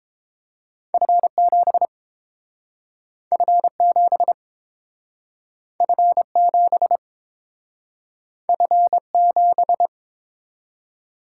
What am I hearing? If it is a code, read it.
Morse code, heard as a keyed tone, about 29 wpm, 700 Hz. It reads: F7 F7 F7 F7